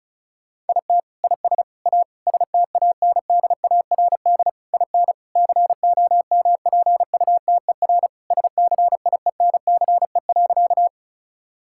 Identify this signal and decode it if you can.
Morse code, heard as a keyed tone, about 36 wpm, 700 Hz. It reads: IT IS A STANDARD IN COMPUTER SCIENCE.
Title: Sentences from Top 1000 Words - Review Part 6 - 35wpm